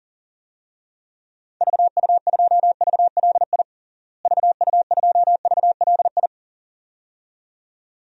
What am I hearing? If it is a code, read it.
Morse code, heard as a keyed tone, about 40 wpm, 700 Hz. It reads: VU2VLI VU2VLI
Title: DXCC-100 Call Signs - 6 Characters - Repeated 2x - 40wpm